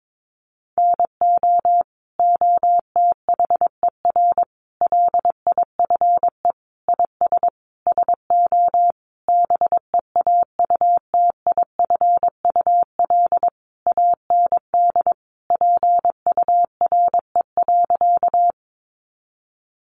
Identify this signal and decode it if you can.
Morse code, heard as a keyed tone, about 22 wpm, 700 Hz. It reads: NO OTHER LIFE IS SO BEAUTIFUL AND PURE.
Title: Sentences from Top 1500 Words 22wpm